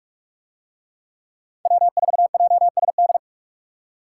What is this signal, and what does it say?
Morse code, heard as a keyed tone, about 45 wpm, 700 Hz. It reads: W4JSD